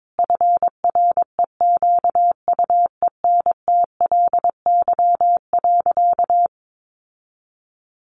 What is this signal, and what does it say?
Morse code, heard as a keyed tone, about 22 wpm, 700 Hz. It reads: FREQUENTLY.
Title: Sentences from Top 1300 Words 22wpm